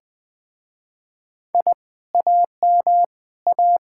Morse code, heard as a keyed tone, about 20 wpm, 700 Hz. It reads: I AM A
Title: Speed-Racing Common QSO Phrases - 17wpm